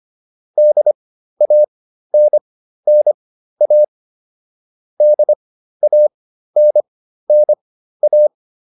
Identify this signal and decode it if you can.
Morse code, heard as a keyed tone, about 25 wpm, spaced as for 13 wpm, 600 Hz. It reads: DANNA DANNA